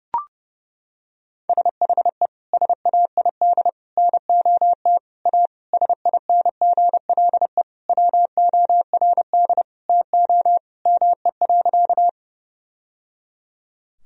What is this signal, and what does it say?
Morse code, heard as a keyed tone, about 30 wpm, 700 Hz. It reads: SHE SAID NOT A SINGLE WORD TO ME.
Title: Sentences from Top 1000 Words - Review Part 4 - 30wpm